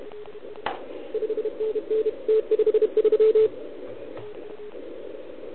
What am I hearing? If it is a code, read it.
Morse code, heard as a keyed tone, about 33 wpm, 430 Hz. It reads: 5NN T53
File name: ARCHIV / OKDX2020 / OL4Wa